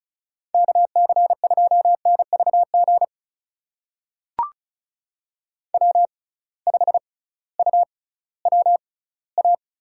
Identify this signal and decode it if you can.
Morse code, heard as a keyed tone, about 36 wpm, spaced as for 19 wpm, 700 Hz. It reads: KC2NVG W5UWA
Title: USA Call Signs - 6x Character Spacing - 35wpm